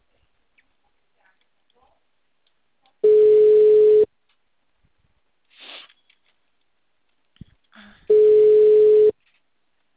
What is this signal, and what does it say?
Ringback tone (Europe, 425 Hz).
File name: out- -2046- - - .6936